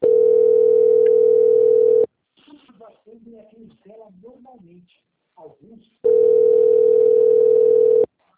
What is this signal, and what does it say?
Ringback tone (North America), 440 + 480 Hz, with speech.
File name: out- - - - - .304